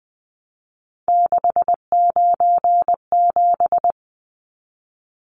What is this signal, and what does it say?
Morse code, heard as a keyed tone, about 20 wpm, 700 Hz. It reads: 697
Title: Numbers - 3-Digit 20wpm